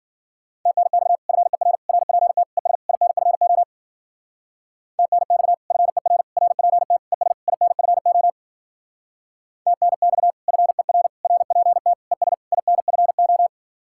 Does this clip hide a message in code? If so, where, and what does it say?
Yes: Morse code from 0.7 to 13.5 s, heard as a keyed tone, about 51 wpm, 700 Hz. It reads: TNX FER RPT ES INFO TNX FER RPT ES INFO TNX FER RPT ES INFO